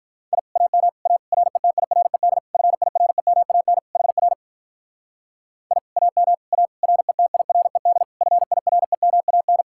Morse code, heard as a keyed tone, about 44 wpm, 700 Hz. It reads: I AM A RETIRED FIREMAN HR I AM A RETIRED FIREMAN